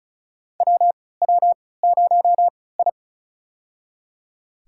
Morse code, heard as a keyed tone, about 35 wpm, spaced as for 20 wpm, 690 Hz. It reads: WW0I
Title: USA Call Signs - 4 Characters - 3x Character Spacing - 35wpm